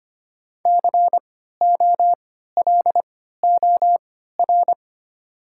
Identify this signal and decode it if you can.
Morse code, heard as a keyed tone, about 25 wpm, spaced as for 14 wpm, 690 Hz. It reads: COLOR
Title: 3x Character Spacing - Top 500 Words - 25wpm